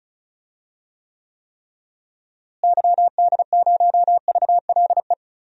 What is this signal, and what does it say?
Morse code, heard as a keyed tone, about 36 wpm, 700 Hz. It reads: YD0VLE
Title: DXCC-100 Call Signs - 6 Characters - Repeated 3x - 35wpm